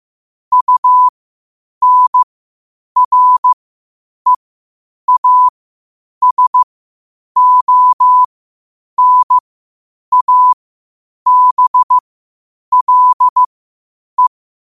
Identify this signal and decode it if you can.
Morse code, heard as a keyed tone, about 15 wpm, spaced as for 8 wpm, 1000 Hz. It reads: UNREASONABLE
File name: qotdfiles / 8wpm